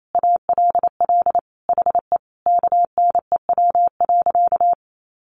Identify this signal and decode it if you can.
Morse code, heard as a keyed tone, about 28 wpm, 700 Hz. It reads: ALL HE KNEW.
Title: Sentences from Top 400 Words 28wpm